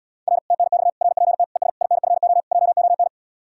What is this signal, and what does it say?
Morse code, heard as a keyed tone, about 79 wpm, 700 Hz. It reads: R TNX RPT ES INFO JON